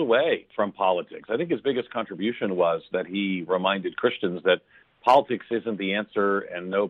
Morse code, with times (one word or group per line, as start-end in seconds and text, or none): none